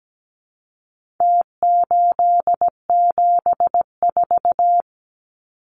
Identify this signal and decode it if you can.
Morse code, heard as a keyed tone, about 17 wpm, 700 Hz. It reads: T874
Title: Numbers - Contest Style - 4-Digit 17wpm